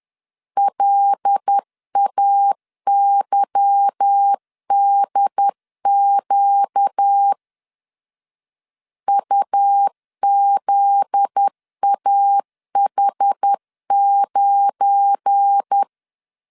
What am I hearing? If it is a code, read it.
Morse code, heard as a keyed tone, about 10 wpm, 800 Hz. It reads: LAYDQ UZAH9
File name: LEZIONI / 8 YJ45 / 8 Y 10wpm 2615